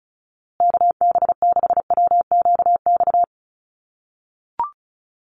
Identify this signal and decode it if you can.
Morse code, heard as a keyed tone, about 34 wpm, 700 Hz. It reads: KB6WQX
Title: USA Call Signs Repeated 3x 35wpm